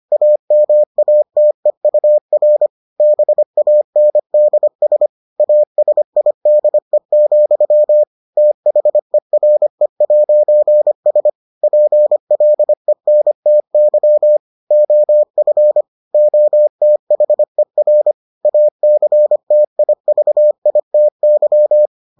Morse code, heard as a keyed tone, about 25 wpm, 600 Hz. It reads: AMATEUR BANDS ASIDE, THERE'S PLENTY OF OTHER ACTIVITY